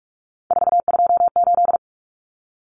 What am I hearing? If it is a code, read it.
Morse code, heard as a keyed tone, about 44 wpm, 700 Hz. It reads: 428
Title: Common Cut Numbers - 3-Digit 45wpm